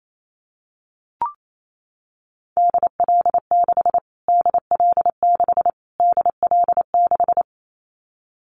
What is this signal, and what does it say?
Morse code, heard as a keyed tone, about 28 wpm, 700 Hz. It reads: DL6 DL6 DL6
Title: Callsign Prefixes for Top 25 DXCC - Repeated 3x - 28wpm